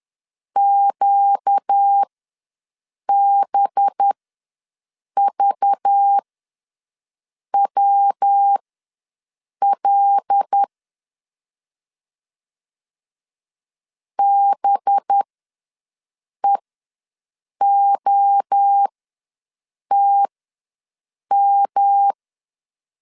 Morse code, heard as a keyed tone, about 10 wpm, spaced as for 6 wpm, 800 Hz. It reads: QBVWL BEOTM
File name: LEZIONI / 5 LBVC / 5 LBV ws 10wpm 3645